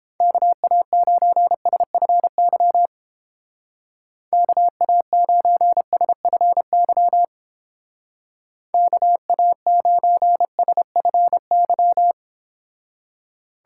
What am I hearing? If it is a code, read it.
Morse code, heard as a keyed tone, about 29 wpm, 700 Hz. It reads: KA9SFY KA9SFY KA9SFY